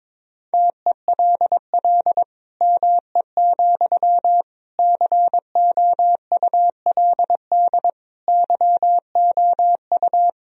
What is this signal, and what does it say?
Morse code, heard as a keyed tone, about 22 wpm, 700 Hz. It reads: TELL ME, COULD YOU
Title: Sentences from Top 1000 Words - Review Part 2 - 22wpm